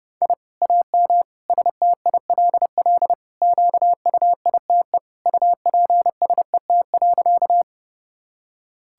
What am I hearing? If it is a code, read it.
Morse code, heard as a keyed tone, about 30 wpm, 700 Hz. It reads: I AM STILL QUITE UPSET.
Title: Sentences from Top 1000 Words - Review Part 5 - 30wpm